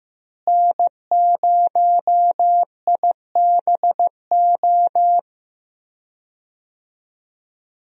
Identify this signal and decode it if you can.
Morse code, heard as a keyed tone, about 15 wpm, 700 Hz. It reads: N0IBO
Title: USA Call Signs - 5 Characters - Repeated 3x - 15wpm